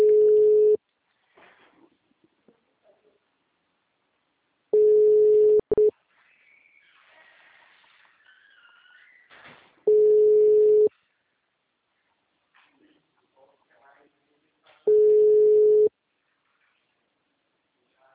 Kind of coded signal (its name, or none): telephone call progress tone